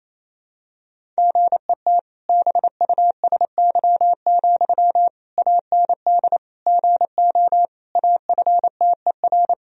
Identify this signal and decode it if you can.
Morse code, heard as a keyed tone, about 28 wpm, 700 Hz. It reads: GET BUSY, AND GO AFTER
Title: Sentences from Top 1000 Words - Review Part 3 - 28wpm